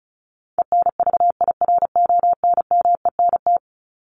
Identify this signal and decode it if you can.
Morse code, heard as a keyed tone, about 34 wpm, 700 Hz. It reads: ENVIRONMENT
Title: Top 501-750 Words 35wpm